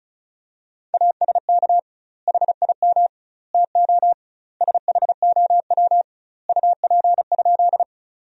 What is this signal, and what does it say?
Morse code, heard as a keyed tone, about 34 wpm, 700 Hz. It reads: ASK HIM TO SHOW UP?
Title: 2x Word-Spacing - Sentences from Top 200 Words - 35wpm